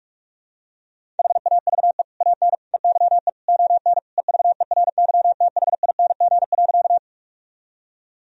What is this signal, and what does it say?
Morse code, heard as a keyed tone, about 46 wpm, 700 Hz. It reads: HAVE AN EYE ON EVERYTHING.
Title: Sentences from Top 1000 Words - Review Part 1 - 45wpm